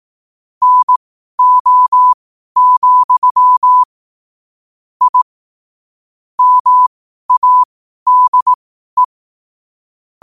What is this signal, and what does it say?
Morse code, heard as a keyed tone, about 18 wpm, spaced as for 12 wpm, 1000 Hz. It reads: NO, I MADE
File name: qotdfiles / 12wpm